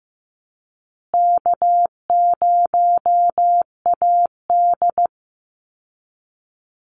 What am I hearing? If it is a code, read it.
Morse code, heard as a keyed tone, about 15 wpm, 700 Hz. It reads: K0AD
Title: Easy Call Signs Repeated 2x - 15wpm